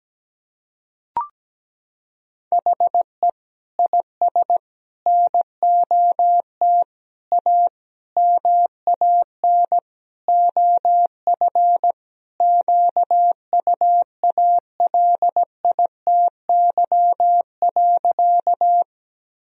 Morse code, heard as a keyed tone, about 17 wpm, 700 Hz. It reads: HE IS NOT A MAN OF QUALITY.
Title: Sentences from Top 1000 Words - Review Part 4 - 17wpm